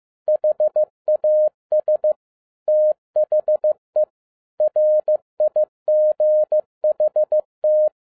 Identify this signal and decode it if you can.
Morse code, heard as a keyed tone, about 15 wpm, 610 Hz. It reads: HAS THE RIGHT
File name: 2012-01-mp3 / n13-A NEWLY DESIGNED QSL CARD-15wpm